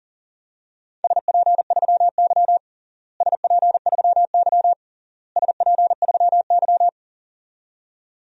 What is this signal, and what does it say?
Morse code, heard as a keyed tone, about 40 wpm, 700 Hz. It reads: SP3Y SP3Y SP3Y
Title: DXCC-100 Call Signs - 4 Characters - Repeated 3x - 40wpm